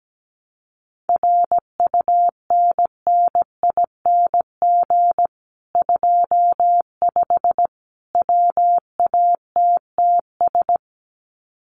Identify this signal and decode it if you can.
Morse code, heard as a keyed tone, about 17 wpm, 700 Hz. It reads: RUNNING 25 WATTS
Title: Common QSO Phrases 17wpm